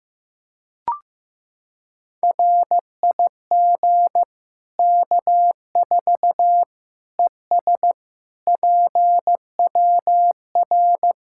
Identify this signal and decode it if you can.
Morse code, heard as a keyed tone, about 16 wpm, 700 Hz. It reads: RIG K4 ES PWR
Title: Practice QSO - Exchange 2 - Info - 15wpm